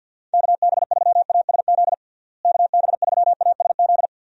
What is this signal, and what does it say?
Morse code, heard as a keyed tone, about 50 wpm, 700 Hz. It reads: KB3ASZ KB3ASZ
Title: USA Call Signs - 6 Characters - Repeated 2x - 50wpm